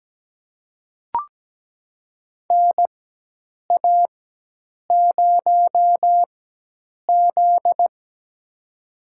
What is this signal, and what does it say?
Morse code, heard as a keyed tone, about 17 wpm, spaced as for 8 wpm, 700 Hz. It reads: NA0Z
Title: USA Call Signs - 4 Characters - 4x Character Spacing - 17wpm